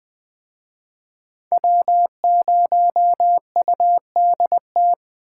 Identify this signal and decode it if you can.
Morse code, heard as a keyed tone, about 20 wpm, 700 Hz. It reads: W0UDT